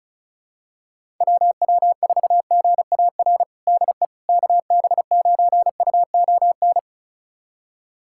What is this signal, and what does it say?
Morse code, heard as a keyed tone, about 34 wpm, 700 Hz. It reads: WW4GAR DE KB9UON
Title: USA Callsign DE Callsign - 6 Characters - 35wpm